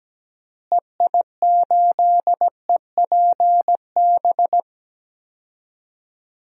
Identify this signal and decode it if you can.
Morse code, heard as a keyed tone, about 17 wpm, 700 Hz. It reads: EI8EPB
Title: DXCC-100 Call Signs - 6 Characters - Repeated 2x - 17wpm